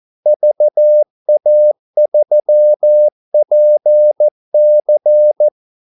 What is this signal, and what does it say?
Morse code, heard as a keyed tone, about 14 wpm, 600 Hz. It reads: VA3PC
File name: Callsigns-14wpm